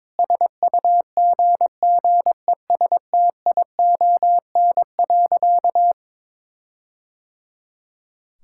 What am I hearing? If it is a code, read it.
Morse code, heard as a keyed tone, about 22 wpm, 700 Hz. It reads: SUGGESTION.
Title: Sentences from Top 1600 Words 22wpm